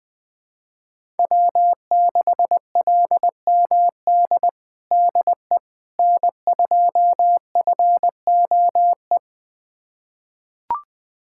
Morse code, heard as a keyed tone, about 21 wpm, 700 Hz. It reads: W6LMD DE N2FOE
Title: USA Callsign DE Callsign - 5 Characters - 20wpm